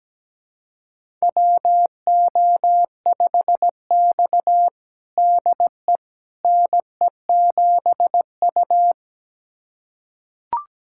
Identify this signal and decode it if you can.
Morse code, heard as a keyed tone, about 17 wpm, 700 Hz. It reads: WO5X DE NE7U
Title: USA Callsign DE Callsign - 4 Characters - 17wpm